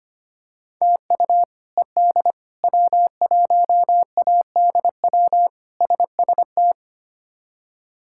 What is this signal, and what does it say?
Morse code, heard as a keyed tone, about 25 wpm, 700 Hz. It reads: TU ED W1ADW SST
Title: K1USN SST QSO Practice - 25wpm